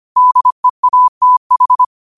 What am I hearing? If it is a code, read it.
Morse code, heard as a keyed tone, about 25 wpm, 1000 Hz. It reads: DEATH